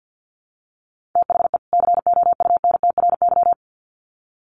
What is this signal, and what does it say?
Morse code, heard as a keyed tone, about 50 wpm, 700 Hz. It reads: THE COUNTRY
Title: Sets of 2 Words 50wpm